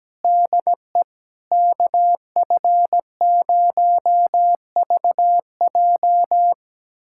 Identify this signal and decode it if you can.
Morse code, heard as a keyed tone, about 17 wpm, 700 Hz. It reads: DE KF0VJ
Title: USA Callsign DE Callsign - 5 Characters - 17wpm